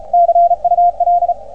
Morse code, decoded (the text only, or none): CUL